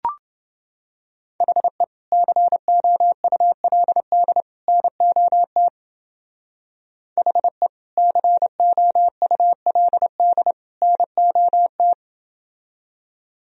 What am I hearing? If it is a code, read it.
Morse code, heard as a keyed tone, about 29 wpm, 700 Hz. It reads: HE COULD NOT HE COULD NOT